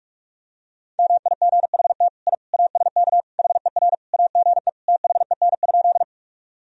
Morse code, heard as a keyed tone, about 46 wpm, 700 Hz. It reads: MIGHT I ASK HER AGE THEN?
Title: Sentences from Top 1000 Words - Review Part 3 - 45wpm